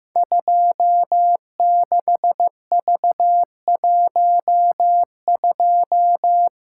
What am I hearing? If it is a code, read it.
Morse code, heard as a keyed tone, about 15 wpm, 700 Hz. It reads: 26V12
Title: Code Groups - 5 Letter:Numbers - 15wpm